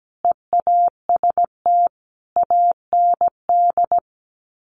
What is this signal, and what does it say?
Morse code, heard as a keyed tone, about 17 wpm, 690 Hz. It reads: EAST AND
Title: Sets of 5 Words Encore 17wpm